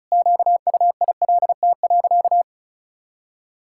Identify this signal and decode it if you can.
Morse code, heard as a keyed tone, about 36 wpm, 700 Hz. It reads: QUILT.
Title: Pangram Challenge v2 - 35wpm